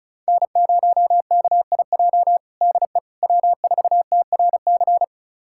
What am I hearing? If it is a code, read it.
Morse code, heard as a keyed tone, about 36 wpm, 700 Hz. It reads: N0KIJ DE W4TRC
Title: USA Callsign DE Callsign - 5 Characters - 35wpm